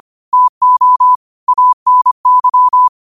Morse code, heard as a keyed tone, about 25 wpm, 1000 Hz. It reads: TO ANY
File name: qotdfiles / 25wpm